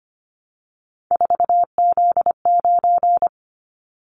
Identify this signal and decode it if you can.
Morse code, heard as a keyed tone, about 25 wpm, 700 Hz. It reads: 4Z9
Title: Callsign Prefixes for Top 50 DXCC - 25wpm